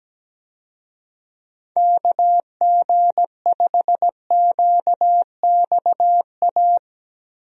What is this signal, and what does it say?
Morse code, heard as a keyed tone, about 17 wpm, 700 Hz. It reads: KG5QXA
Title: USA Call Signs - 6 Characters - Repeated 3x - 17wpm